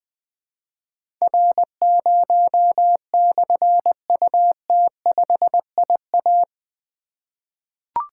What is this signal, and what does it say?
Morse code, heard as a keyed tone, about 20 wpm, 700 Hz. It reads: R0/UT5IA
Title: Difficult Call Signs 20wpm